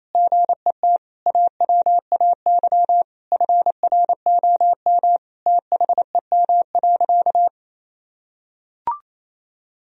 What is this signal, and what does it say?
Morse code, heard as a keyed tone, about 28 wpm, 690 Hz. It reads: GET AWAY FROM THEM.
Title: Sentences from Top 1000 Words - Review Part 2 - 28wpm